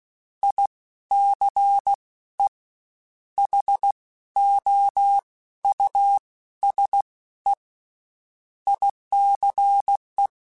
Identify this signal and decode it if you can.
Morse code, heard as a keyed tone, about 16 wpm, 800 Hz. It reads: I C E H O U S E ICE
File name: General / General-mp3s / Houses @15